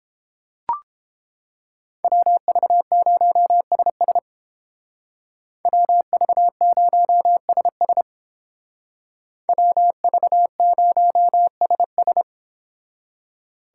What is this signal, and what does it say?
Morse code, heard as a keyed tone, about 30 wpm, 700 Hz. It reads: WV0SS WV0SS WV0SS